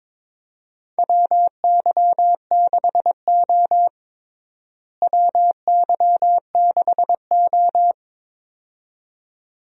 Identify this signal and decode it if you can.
Morse code, heard as a keyed tone, about 22 wpm, 700 Hz. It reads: WY6O WY6O
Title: USA Call Signs - 4 Characters - Repeated 2x - 22wpm